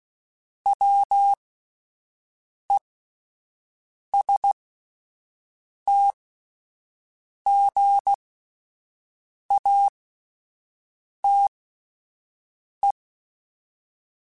Morse code, heard as a keyed tone, about 16 wpm, spaced as for 5 wpm, 800 Hz. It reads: WESTGATE